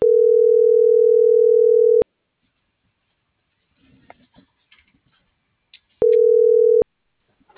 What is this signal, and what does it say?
Ringback tone (North America), 440 + 480 Hz.